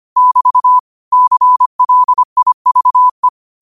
Morse code, heard as a keyed tone, about 25 wpm, 1000 Hz. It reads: <BT> CLIVE
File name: qotdfiles / 25wpm